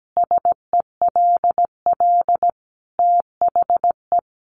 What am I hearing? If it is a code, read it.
Morse code, heard as a keyed tone, about 17 wpm, 700 Hz. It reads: SELL THE
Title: Sentences from Top 1200 Words 17wpm